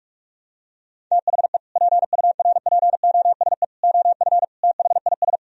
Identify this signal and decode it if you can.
Morse code, heard as a keyed tone, about 44 wpm, 700 Hz. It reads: THE PURPOSE OF THIS